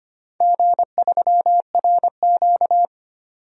Morse code, heard as a keyed tone, about 25 wpm, 700 Hz. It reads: G3RQ
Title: DXCC-100 Call Signs - 4 Characters - 25wpm